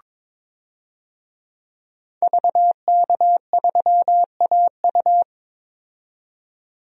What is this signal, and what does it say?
Morse code, heard as a keyed tone, about 22 wpm, 700 Hz. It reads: VK3AU
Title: DXCC-100 Call Signs - 5 Characters - 22wpm